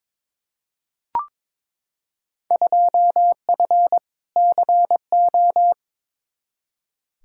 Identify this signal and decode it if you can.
Morse code, heard as a keyed tone, about 22 wpm, 700 Hz. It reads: 2F CO
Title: Field Day Exchanges - 22wpm